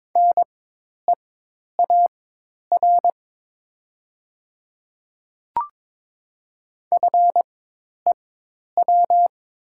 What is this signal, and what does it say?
Morse code, heard as a keyed tone, about 22 wpm, spaced as for 10 wpm, 700 Hz. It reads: NEAR FEW
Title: 4x Character Spacing - Top 300 Words - 22wpm